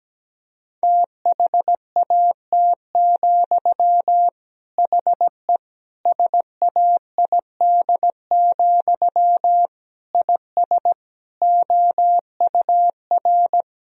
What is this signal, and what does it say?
Morse code, heard as a keyed tone, about 17 wpm, 700 Hz. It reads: THAT, HE SAID, IS OUR
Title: Sentences from Top 1000 Words - Review Part 3 - 17wpm